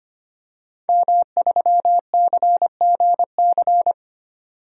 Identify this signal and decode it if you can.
Morse code, heard as a keyed tone, about 25 wpm, 700 Hz. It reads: M3CGC
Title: DXCC-100 Call Signs - 5 Characters - 25wpm